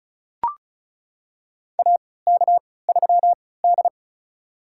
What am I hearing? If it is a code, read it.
Morse code, heard as a keyed tone, about 34 wpm, spaced as for 20 wpm, 700 Hz. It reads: AK3D
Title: USA Call Signs - 4 Characters - 3x Character Spacing - 35wpm